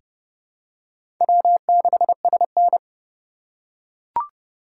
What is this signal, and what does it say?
Morse code, heard as a keyed tone, about 30 wpm, 700 Hz. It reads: W6SN
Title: USA Call Signs - 3x Character Spacing - 30wpm